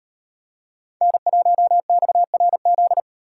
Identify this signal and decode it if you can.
Morse code, heard as a keyed tone, about 33 wpm, 700 Hz. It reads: N1XRZ